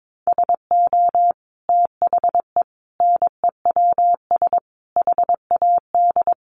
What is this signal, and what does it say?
Morse code, heard as a keyed tone, about 22 wpm, 700 Hz. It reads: SO THE NEWS HAD
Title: Sentences from Top 1200 Words 22wpm